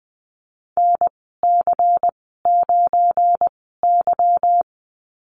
Morse code, heard as a keyed tone, about 20 wpm, spaced as for 14 wpm, 700 Hz. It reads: NC9Y
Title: 200 Lesson 32g - Letter k - Call Signs 20wpm